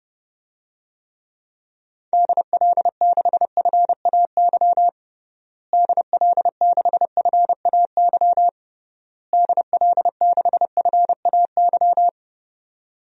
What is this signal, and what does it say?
Morse code, heard as a keyed tone, about 30 wpm, 700 Hz. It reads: DL6FAY DL6FAY DL6FAY
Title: DXCC-100 Call Signs - 6 Characters - Repeated 3x - 30wpm